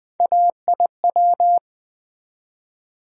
Morse code, heard as a keyed tone, about 20 wpm, 700 Hz. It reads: AIW
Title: USA Call Sign Postfixes - 3 Letters 20wpm